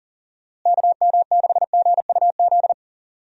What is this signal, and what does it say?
Morse code, heard as a keyed tone, about 37 wpm, 700 Hz. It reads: KM6GUZ
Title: USA Call Signs - 6 Characters - Repeated 2x - 40wpm